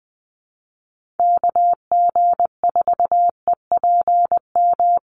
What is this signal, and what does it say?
Morse code, heard as a keyed tone, about 20 wpm, 700 Hz. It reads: KG4EPM
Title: USA Call Signs Repeated 2x 20wpm